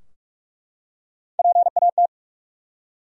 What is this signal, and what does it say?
Morse code, heard as a keyed tone, about 43 wpm, 700 Hz. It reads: PAT
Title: USA Common Names - Max 3 Letters - 45wpm